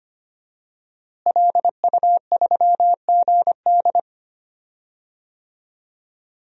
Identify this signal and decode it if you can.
Morse code, heard as a keyed tone, about 25 wpm, 700 Hz. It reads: LU3GD
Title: DXCC-100 Call Signs - 5 Characters - Repeated 2x - 25wpm